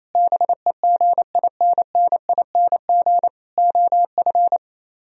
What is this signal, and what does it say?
Morse code, heard as a keyed tone, about 26 wpm, 700 Hz. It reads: BEGINNING OF
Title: Speed-Racing - Sets of 3 Words 28wpm